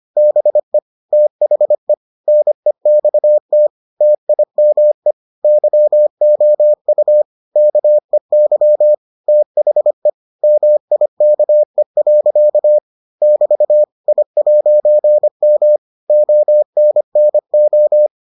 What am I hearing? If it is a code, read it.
Morse code, heard as a keyed tone, about 25 wpm, 600 Hz. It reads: BE THE NEXT TIME YOU KEY THE MIKE. <BT> I'M ONNO